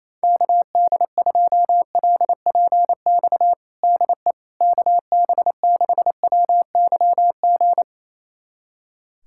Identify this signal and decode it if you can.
Morse code, heard as a keyed tone, about 28 wpm, 700 Hz. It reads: KD2LPX DE KB6WYG
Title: USA Callsign DE Callsign - 6 Characters - 28wpm